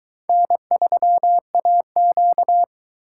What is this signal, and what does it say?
Morse code, heard as a keyed tone, about 23 wpm, 700 Hz. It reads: N3AQ